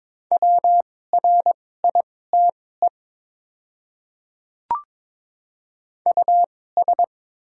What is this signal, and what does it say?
Morse code, heard as a keyed tone, about 22 wpm, spaced as for 16 wpm, 700 Hz. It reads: WRITE US
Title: 2x Character Spacing - Top 200 Words - 22wpm